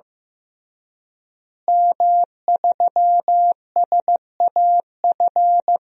Morse code, heard as a keyed tone, about 15 wpm, 700 Hz. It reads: M3SAF
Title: DXCC-100 Call Signs - 5 Characters - Repeated 3x - 15wpm